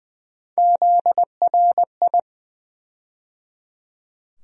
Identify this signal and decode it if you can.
Morse code, heard as a keyed tone, about 20 wpm, 700 Hz. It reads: ZRI
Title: USA Call Sign Postfixes - 3 Letters 20wpm